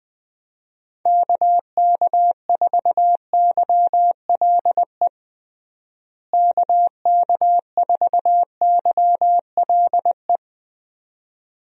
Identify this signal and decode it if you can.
Morse code, heard as a keyed tone, about 21 wpm, 700 Hz. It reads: KK4YLE KK4YLE
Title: USA Call Signs - 6 Characters - Repeated 2x - 20wpm